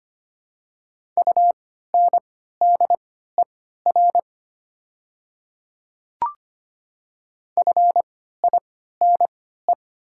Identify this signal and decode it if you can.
Morse code, heard as a keyed tone, about 25 wpm, spaced as for 14 wpm, 700 Hz. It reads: UNDER FINE
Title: 3x Character Spacing - Top 500 Words - 25wpm